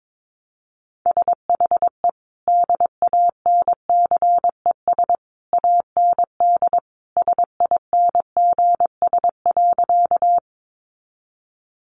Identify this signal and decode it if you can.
Morse code, heard as a keyed tone, about 22 wpm, 700 Hz. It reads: SHE DANCES AND SINGS.
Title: Easy Phrases 22wpm